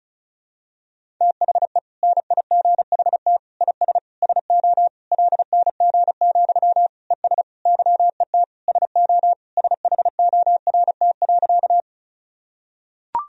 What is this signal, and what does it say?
Morse code, heard as a keyed tone, about 36 wpm, 700 Hz. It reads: THE NIGHT IS SO LNG, ES YET SO SHORT.
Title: Sentences from Top 500 Words and Abbreviations 35wpm